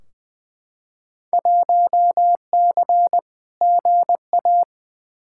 Morse code, heard as a keyed tone, about 20 wpm, 700 Hz. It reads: 1C GA
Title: Field Day Exchanges - 20wpm